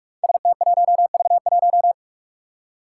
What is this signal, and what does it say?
Morse code, heard as a keyed tone, about 46 wpm, 700 Hz. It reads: ST1V1
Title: Five Challenging Letter-Number Combinations 45wpm